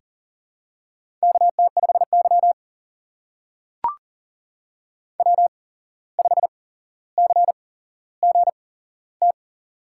Morse code, heard as a keyed tone, about 38 wpm, spaced as for 16 wpm, 700 Hz. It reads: KT5Y W5CGT
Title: USA Call Signs - 8x Character Spacing - 40wpm